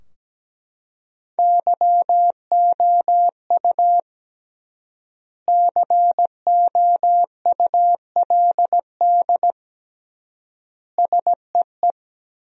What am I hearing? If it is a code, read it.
Morse code, heard as a keyed tone, about 17 wpm, 700 Hz. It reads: YOU COULD SEE
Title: 3x Word-Spacing - Sentences from Top 500 Words - 17wpm